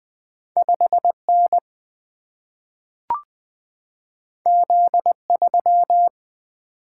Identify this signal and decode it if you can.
Morse code, heard as a keyed tone, about 20 wpm, 700 Hz. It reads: 5 N Z 3
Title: Single-Letter Single-Number 20wpm